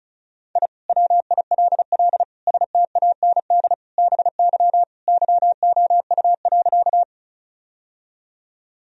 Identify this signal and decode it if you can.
Morse code, heard as a keyed tone, about 34 wpm, 700 Hz. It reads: I WILL STAND BY YOU.
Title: Sentences from Top 1000 Words - Review Part 1 - 35wpm